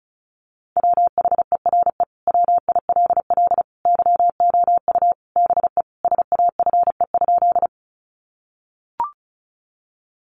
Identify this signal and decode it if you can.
Morse code, heard as a keyed tone, about 34 wpm, 700 Hz. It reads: WHERE WILL YOU BE SAFE?
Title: Sentences from Top 900 Words 35wpm